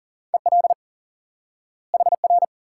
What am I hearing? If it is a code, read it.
Morse code, heard as a keyed tone, about 40 wpm, 700 Hz. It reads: EL HR
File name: IQE Course 3 - Lesson 89 - YAGI - Two Elements - 40wpm